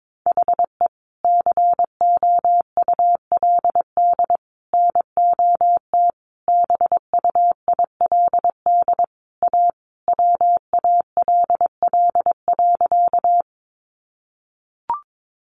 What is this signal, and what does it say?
Morse code, heard as a keyed tone, about 22 wpm, 700 Hz. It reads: HE COULD NOT BUILD A WALL.
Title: Sentences from Top 1100 Words 22wpm